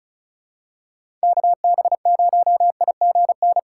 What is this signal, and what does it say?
Morse code, heard as a keyed tone, about 36 wpm, 700 Hz. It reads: KB0IGN
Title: USA Call Signs - 4x Character Spacing - 35wpm